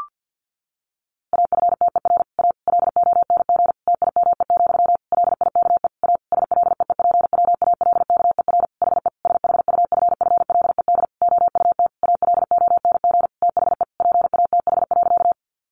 Morse code, heard as a keyed tone, about 50 wpm, 700 Hz. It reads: AFTER A LONG TIME, LIKE A SLEEPWALKER HE SHUFFLED OUT ALONG THE PATH.
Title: A Sound Of Thunder 50wpm